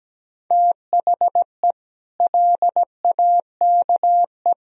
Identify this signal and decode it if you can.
Morse code, heard as a keyed tone, about 17 wpm, 700 Hz. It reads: THE LAKE
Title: Sentences from Top 1000 Words - Review Part 3 - 17wpm